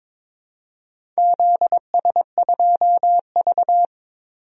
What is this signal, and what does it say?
Morse code, heard as a keyed tone, about 22 wpm, 700 Hz. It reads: ZS2V
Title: DXCC-100 Call Signs - 4 Characters - 22wpm